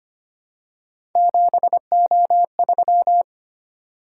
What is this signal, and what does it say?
Morse code, heard as a keyed tone, about 25 wpm, 700 Hz. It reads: 7O3
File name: Three Letter-Number Combinations - 25wpm